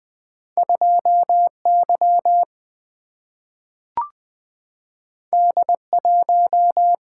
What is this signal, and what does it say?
Morse code, heard as a keyed tone, about 20 wpm, 700 Hz. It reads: 2Y D1
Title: Single-Letter Single-Number 20wpm